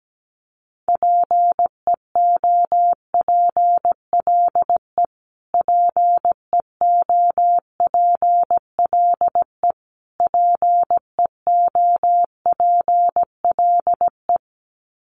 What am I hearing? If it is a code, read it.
Morse code, heard as a keyed tone, about 17 wpm, 700 Hz. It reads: PEOPLE PEOPLE PEOPLE
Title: Top 500 Words Repeated 17wpm